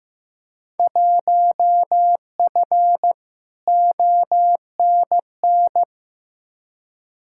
Morse code, heard as a keyed tone, about 16 wpm, 700 Hz. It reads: 1F ONN
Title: Field Day Exchanges Encore - 15wpm